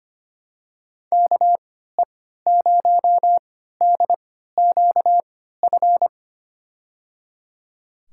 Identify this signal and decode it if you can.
Morse code, heard as a keyed tone, about 25 wpm, spaced as for 14 wpm, 700 Hz. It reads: KE0DQF
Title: USA Call Signs - 6 Characters - 3x Character Spacing - 25wpm